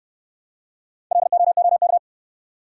Morse code, heard as a keyed tone, about 68 wpm, 700 Hz. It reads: LOOK